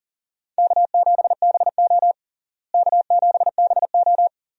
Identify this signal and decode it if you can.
Morse code, heard as a keyed tone, about 37 wpm, 700 Hz. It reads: K7BO K7BO
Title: USA Call Signs - 4 Characters - Repeated 3x - 40wpm